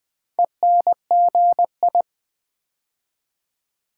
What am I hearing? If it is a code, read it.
Morse code, heard as a keyed tone, about 20 wpm, 700 Hz. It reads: ENGI
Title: Build A Word - 4 Letters At A Time - 8-Letter Words - 20wpm